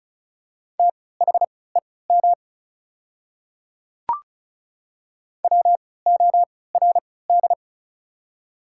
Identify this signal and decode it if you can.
Morse code, heard as a keyed tone, about 36 wpm, spaced as for 20 wpm, 690 Hz. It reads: THEM WORD
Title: 3x Character Spacing - Top 100 Words - 35wpm